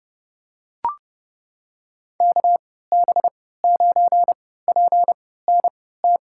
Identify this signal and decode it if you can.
Morse code, heard as a keyed tone, about 31 wpm, spaced as for 17 wpm, 700 Hz. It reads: KB9PNT
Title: USA Call Signs - 6 Characters - 3x Character Spacing - 30wpm